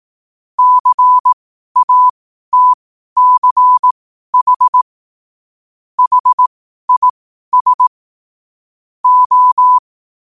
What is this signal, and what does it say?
Morse code, heard as a keyed tone, about 18 wpm, spaced as for 12 wpm, 1000 Hz. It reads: CATCH HIS O
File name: qotdfiles / 12wpm